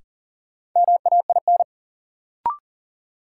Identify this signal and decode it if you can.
Morse code, heard as a keyed tone, about 40 wpm, 700 Hz. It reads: MAIN
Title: Sentences from Top 1000 Words - Review Part 2 - New Word Drill - 40wpm